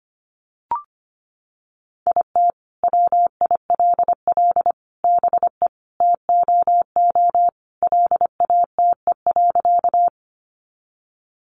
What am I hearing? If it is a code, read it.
Morse code, heard as a keyed tone, about 25 wpm, 700 Hz. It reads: IT WILL BE TOO LATE.
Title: Sentences from Top 300 Words 25wpm